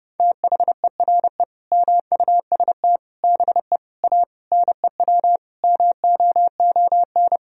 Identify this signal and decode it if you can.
Morse code, heard as a keyed tone, about 32 wpm, 700 Hz. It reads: THERE MUST BE A NEW MOON
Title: Sentences from Top 1000 Words - Review Part 3 - 30wpm